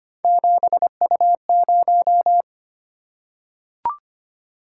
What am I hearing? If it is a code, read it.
Morse code, heard as a keyed tone, about 25 wpm, 700 Hz. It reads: 7U0
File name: Three Letter-Number Combinations - 25wpm